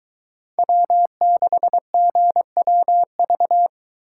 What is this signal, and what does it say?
Morse code, heard as a keyed tone, about 22 wpm, 700 Hz. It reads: W6GWV